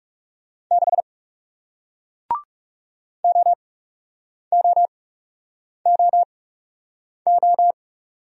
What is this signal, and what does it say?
Morse code, heard as a keyed tone, about 40 wpm, spaced as for 8 wpm, 700 Hz. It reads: 6 OOOO